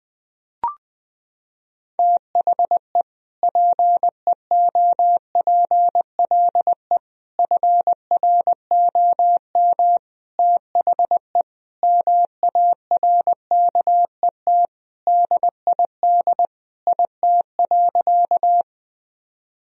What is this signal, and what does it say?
Morse code, heard as a keyed tone, about 21 wpm, 700 Hz. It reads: THE PEOPLE FROM THE MARKET DID IT.
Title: Sentences from Top 1000 Words - Review Part 3 - 20wpm